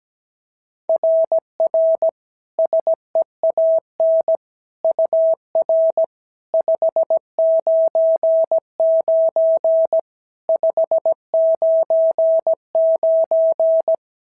Morse code, heard as a keyed tone, about 15 wpm, 650 Hz. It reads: RR SEAN UR 599 599
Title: SOTA QSO Practice 17wpm